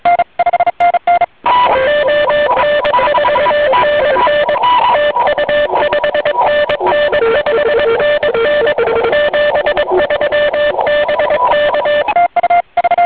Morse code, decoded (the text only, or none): ON4NN TU 5NN 5NN 73 BK